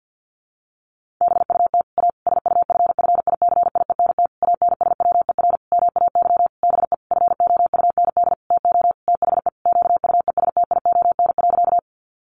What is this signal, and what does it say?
Morse code, heard as a keyed tone, about 50 wpm, 700 Hz. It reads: BUT A SUFFICIENT ANSWER MAY BE FOUND TO THE QUESTION.
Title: Sentences from Top 1400 Words 50wpm